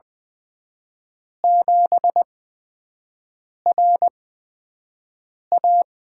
Morse code, heard as a keyed tone, about 21 wpm, spaced as for 5 wpm, 700 Hz. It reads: 7RA
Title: USA Call Signs - 8x Character Spacing - 20wpm